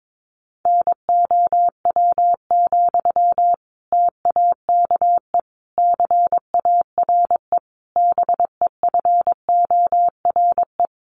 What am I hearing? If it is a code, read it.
Morse code, heard as a keyed tone, about 22 wpm, 700 Hz. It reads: NOW, TAKE CARE BEFORE
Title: Sentences from Top 1500 Words 22wpm